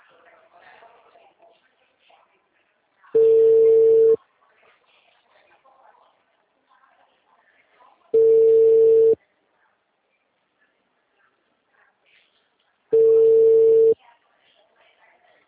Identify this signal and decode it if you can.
Ringback tone (China, 450 Hz), with speech.